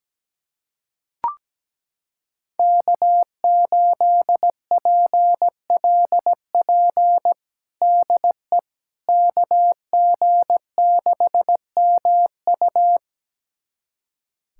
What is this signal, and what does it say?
Morse code, heard as a keyed tone, about 17 wpm, 700 Hz. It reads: K8PLP DE KG6MU
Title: USA Callsign DE Callsign - 5 Characters - 17wpm